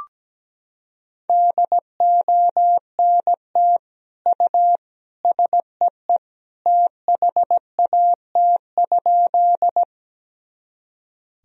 Morse code, heard as a keyed tone, about 17 wpm, 700 Hz. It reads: DONT U SEE THAT?
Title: Sentences from Top 300 Words and Abbreviations 17wpm